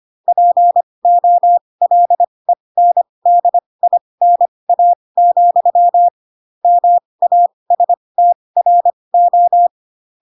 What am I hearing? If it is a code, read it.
Morse code, heard as a keyed tone, about 25 wpm, spaced as for 20 wpm, 700 Hz. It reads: POLENDINA, MASTRO
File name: Pinocchio / Pinocchio ch2 15 25 ik1hge